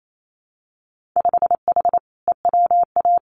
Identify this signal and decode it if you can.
Morse code, heard as a keyed tone, about 28 wpm, 700 Hz. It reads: 5H EWA
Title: Winter Field Day Exchanges Encore 28wpm